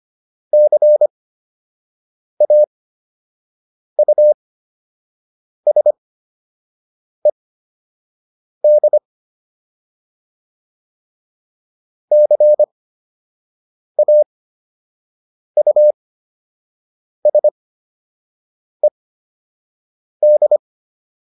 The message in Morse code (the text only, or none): CAUSED CAUSED